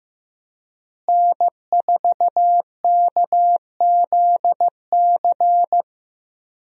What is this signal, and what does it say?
Morse code, heard as a keyed tone, about 15 wpm, 700 Hz. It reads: N4KZC
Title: USA Call Signs - 5 Characters - Repeated 3x - 15wpm